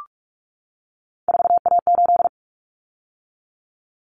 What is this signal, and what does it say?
Morse code, heard as a keyed tone, about 44 wpm, 700 Hz. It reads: 4A8
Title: Common Cut Numbers - 3-Digit 45wpm